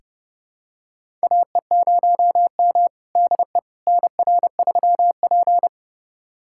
Morse code, heard as a keyed tone, about 30 wpm, 700 Hz. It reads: AE0M DE NR3P
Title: USA Callsign DE Callsign - 4 Characters - 30wpm